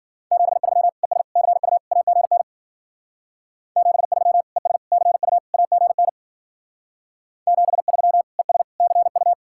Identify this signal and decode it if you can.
Morse code, heard as a keyed tone, about 49 wpm, 700 Hz. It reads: 73 ES CU AGN 73 ES CU AGN 73 ES CU
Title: Speed-Racing Common QSO Phrases - 40wpm